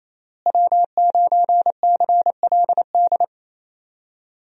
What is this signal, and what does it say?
Morse code, heard as a keyed tone, about 28 wpm, 700 Hz. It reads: W9CLD
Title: USA Call Signs - 5 Characters - Repeated 2x - 28wpm